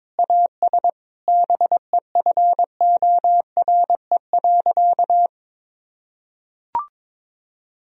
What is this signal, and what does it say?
Morse code, heard as a keyed tone, about 22 wpm, 700 Hz. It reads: AS BEFORE.
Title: Sentences from Top 1000 Words - Review Part 2 - 22wpm